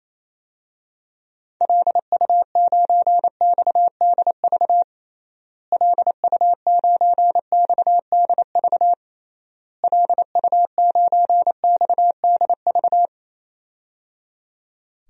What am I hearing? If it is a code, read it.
Morse code, heard as a keyed tone, about 28 wpm, 700 Hz. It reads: LU9XDV LU9XDV LU9XDV
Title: DXCC-100 Call Signs - 6 Characters - Repeated 3x - 28wpm